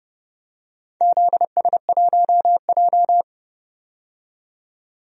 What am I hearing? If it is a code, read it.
Morse code, heard as a keyed tone, about 30 wpm, 700 Hz. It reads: ZS1J
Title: DXCC-100 Call Signs - 4 Characters - 30wpm